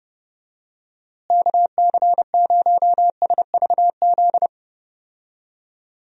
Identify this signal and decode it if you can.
Morse code, heard as a keyed tone, about 31 wpm, 700 Hz. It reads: KC0SVZ
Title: USA Call Signs - 6 Characters - Repeated 3x - 30wpm